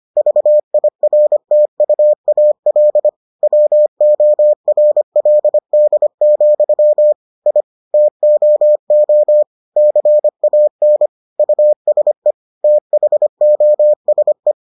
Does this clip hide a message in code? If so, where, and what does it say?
Yes: Morse code from 0.2 to 14.5 s, heard as a keyed tone, about 25 wpm, 600 Hz. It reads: VIRTUAL WORLD, I TOO CAN USE THOSE